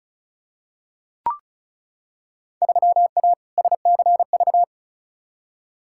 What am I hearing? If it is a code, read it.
Morse code, heard as a keyed tone, about 36 wpm, 700 Hz. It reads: 3A SCV
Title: Field Day Exchanges - 35wpm